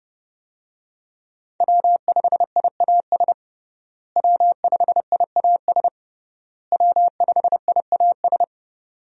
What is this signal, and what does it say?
Morse code, heard as a keyed tone, about 30 wpm, 700 Hz. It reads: W5IAS W5IAS W5IAS
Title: USA Call Signs - 5 Characters - Repeated 3x - 30wpm